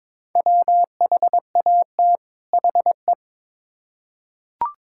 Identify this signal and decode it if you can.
Morse code, heard as a keyed tone, about 22 wpm, 700 Hz. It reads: WHAT HE
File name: Build A Phrase - 2-Word Phrases - 22wpm